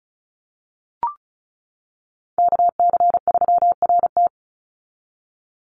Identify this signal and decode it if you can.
Morse code, heard as a keyed tone, about 34 wpm, 700 Hz. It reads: KC3RT
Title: NA SOTA Chaser Callsigns 35wpm